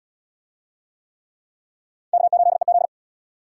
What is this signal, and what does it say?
Morse code, heard as a keyed tone, about 77 wpm, 700 Hz. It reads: K8EZ